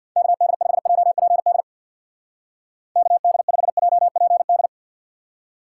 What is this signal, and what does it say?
Morse code, heard as a keyed tone, about 61 wpm, 700 Hz. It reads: KD5JPD KD5JPD
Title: Speed Racing - USA 2x3 Call Signs 50wpm